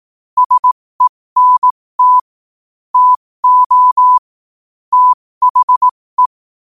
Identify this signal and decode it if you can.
Morse code, heard as a keyed tone, about 18 wpm, spaced as for 15 wpm, 1000 Hz. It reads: SENT TO THE
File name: qotdfiles / 15wpm